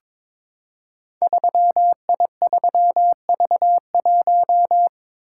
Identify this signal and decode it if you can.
Morse code, heard as a keyed tone, about 22 wpm, 700 Hz. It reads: 3I3V1
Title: Code Groups - 5 Letter:Numbers - 22wpm